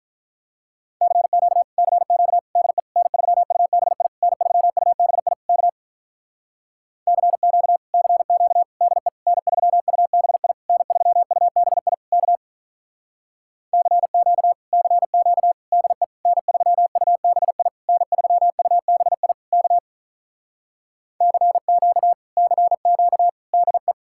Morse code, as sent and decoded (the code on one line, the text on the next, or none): -.-. --.- / -.-. --.- / -.. / -. ...-- ..- -... .. / -. ...-- ..- -... .. / -.- / -.-. --.- / -.-. --.- / -.. / -. ...-- ..- -... .. / -. ...-- ..- -... .. / -.- / -.-. --.- / -.-. --.- / -.. / -. ...-- ..- -... .. / -. ...-- ..- -... .. / -.- / -.-. --.- / -.-. --.- / -.. .
CQ CQ DE N3UBI N3UBI K CQ CQ DE N3UBI N3UBI K CQ CQ DE N3UBI N3UBI K CQ CQ DE